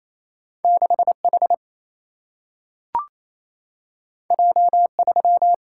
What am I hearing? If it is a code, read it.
Morse code, heard as a keyed tone, about 28 wpm, 700 Hz. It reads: 6 H J 3
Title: Single-Letter Single-Number 28wpm